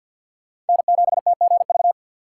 Morse code, heard as a keyed tone, about 53 wpm, 700 Hz. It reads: N7TGV
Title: USA Call Signs - 5 Characters - 2x Character Spacing - 50wpm